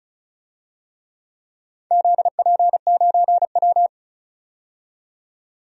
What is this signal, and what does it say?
Morse code, heard as a keyed tone, about 36 wpm, 700 Hz. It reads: ZP9W
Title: DXCC-100 Call Signs - 4 Characters - 35wpm